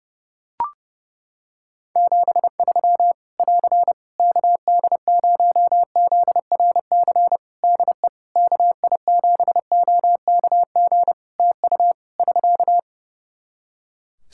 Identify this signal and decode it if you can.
Morse code, heard as a keyed tone, about 32 wpm, 700 Hz. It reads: 73 <AR> KD0ZRC DE KI7OKG TU <SK>
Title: K1USN SST QSO Practice 30wpm